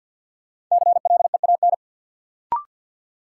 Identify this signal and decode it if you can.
Morse code, heard as a keyed tone, about 53 wpm, 700 Hz. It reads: CLEAN
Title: Sentences from Top 1100 Words - New Word Drill - 50wpm